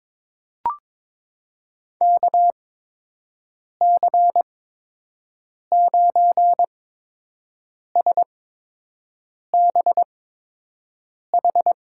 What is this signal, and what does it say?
Morse code, heard as a keyed tone, about 21 wpm, spaced as for 6 wpm, 700 Hz. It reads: KC9SBH
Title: USA Call Signs - 8x Character Spacing - 22wpm